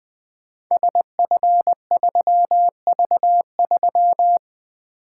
Morse code, heard as a keyed tone, about 20 wpm, 700 Hz. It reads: SF3V3
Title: Code Groups - 5 Letter:Numbers - 20wpm